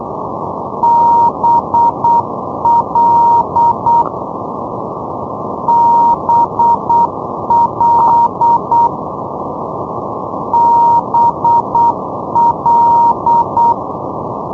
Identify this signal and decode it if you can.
Morse code, heard as a keyed tone, about 8 wpm, 940 Hz, with noise. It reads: BL BL BL
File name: europe / isl / isl-203 ndb blonduos bl-351-isl-2012-04-rb